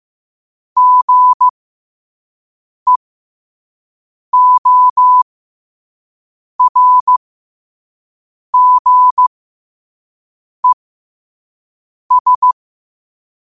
Morse code, heard as a keyed tone, about 15 wpm, spaced as for 5 wpm, 1000 Hz. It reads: GEORGES
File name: qotdfiles / 5wpm